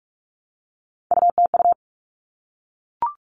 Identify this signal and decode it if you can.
Morse code, heard as a keyed tone, about 44 wpm, 700 Hz. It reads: UTU
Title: All Cut Numbers - 3-Digit 45wpm